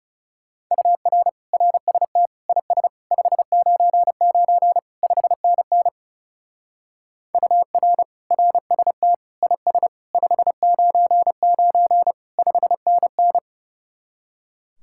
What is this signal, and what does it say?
Morse code, heard as a keyed tone, about 40 wpm, 700 Hz. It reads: UR RST IS 599 5NN UR RST IS 599 5NN